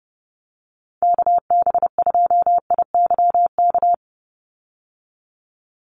Morse code, heard as a keyed tone, about 30 wpm, 700 Hz. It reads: KB2IYK
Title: USA Call Signs 30wpm